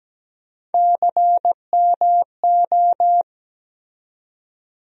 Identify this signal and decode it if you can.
Morse code, heard as a keyed tone, about 17 wpm, 700 Hz. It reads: CMO
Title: USA Call Sign Postfixes - 3 Letters 17wpm